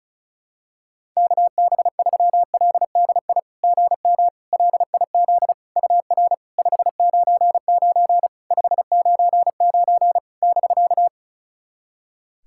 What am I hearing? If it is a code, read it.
Morse code, heard as a keyed tone, about 34 wpm, 690 Hz. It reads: KB3LDI GM LIZ UR 599 599 <BK>
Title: SOTA QSO Practice 35wpm